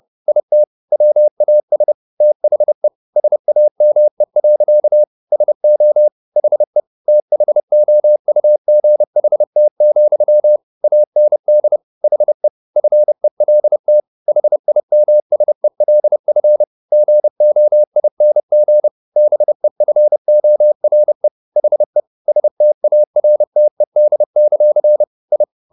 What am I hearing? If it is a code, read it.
Morse code, heard as a keyed tone, about 30 wpm, 600 Hz. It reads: IT WAS THE SAME. SO HE THOUGHT, AND HE FELT HIMSELF GOING BEFORE HE STARTED; I